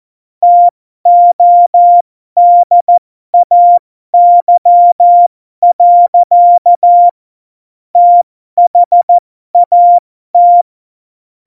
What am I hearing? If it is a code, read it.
Morse code, heard as a keyed tone, about 14 wpm, spaced as for 12 wpm, 700 Hz. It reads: TODAY. THAT
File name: Morsekurs / Witze